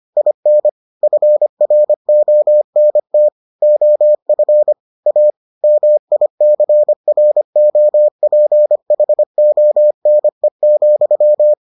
Morse code, heard as a keyed tone, about 25 wpm, 600 Hz. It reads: IN FRONT OF A MICROPHONE,